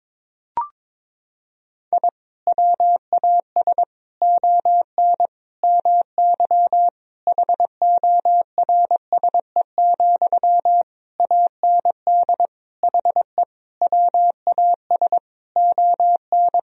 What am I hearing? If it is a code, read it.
Morse code, heard as a keyed tone, about 21 wpm, 700 Hz. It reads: I WAS ON MY HORSE, AND HE WAS ON
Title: Sentences from Top 1000 Words - Review Part 3 - 22wpm